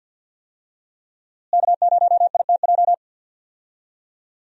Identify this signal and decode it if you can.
Morse code, heard as a keyed tone, about 50 wpm, 700 Hz. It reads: K0ITJ
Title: USA Call Signs - 5 Characters - Repeated 3x - 50wpm